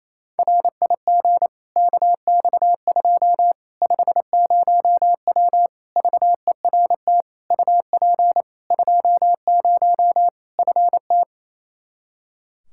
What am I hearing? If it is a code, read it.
Morse code, heard as a keyed tone, about 28 wpm, 700 Hz. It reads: RIG KX2 50W VERT UP 20 FT
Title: Practice QSO - Exchange 2 - Info - 28wpm